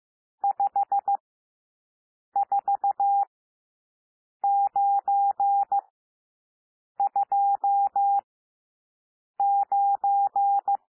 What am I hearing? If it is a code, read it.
Morse code, heard as a keyed tone, about 15 wpm, spaced as for 6 wpm, 800 Hz. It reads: 54929
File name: cwmp3 / N1 SOUND NUMBERS 15 wpm